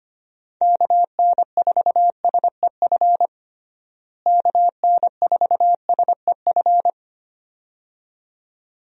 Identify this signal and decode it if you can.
Morse code, heard as a keyed tone, about 25 wpm, 700 Hz. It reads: KN4SEF KN4SEF
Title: USA Call Signs - 6 Characters - Repeated 2x - 25wpm